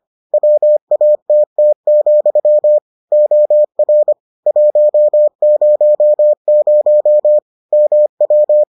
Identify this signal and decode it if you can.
Morse code, heard as a keyed tone, about 25 wpm, 600 Hz. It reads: WATT, OR 100 MW